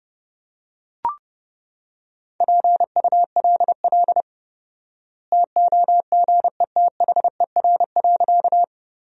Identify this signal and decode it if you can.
Morse code, heard as a keyed tone, about 29 wpm, 700 Hz. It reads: PULL TOGETHER.
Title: 4x Word-Spacing - Sentences from Top 400 Words - 30wpm